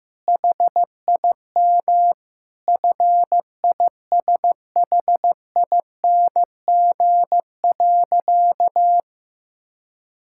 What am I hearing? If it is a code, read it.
Morse code, heard as a keyed tone, about 14 wpm, 700 Hz. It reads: HIM FISHING.
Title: Sentences from Top 1000 Words - Review Part 5 - 15wpm